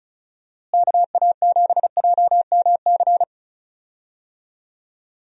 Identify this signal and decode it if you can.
Morse code, heard as a keyed tone, about 36 wpm, 700 Hz. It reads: KA7JMC
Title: USA Call Signs - 6 Characters - Repeated 2x - 35wpm